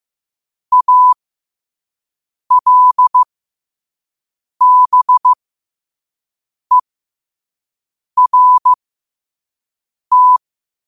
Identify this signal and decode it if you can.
Morse code, heard as a keyed tone, about 15 wpm, spaced as for 5 wpm, 1000 Hz. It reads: ALBERT
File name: qotdfiles / 5wpm